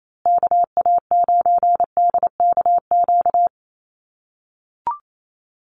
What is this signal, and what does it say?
Morse code, heard as a keyed tone, about 28 wpm, 700 Hz. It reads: KA9DKQ
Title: USA Call Signs Repeated 3x 28wpm